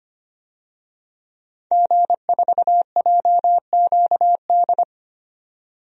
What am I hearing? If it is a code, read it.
Morse code, heard as a keyed tone, about 25 wpm, 700 Hz. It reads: G4JQD
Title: DXCC-100 Call Signs - 5 Characters - 25wpm